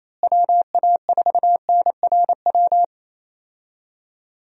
Morse code, heard as a keyed tone, about 28 wpm, 700 Hz. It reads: WA4NRW
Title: USA Call Signs - 6 Characters - Repeated 3x - 28wpm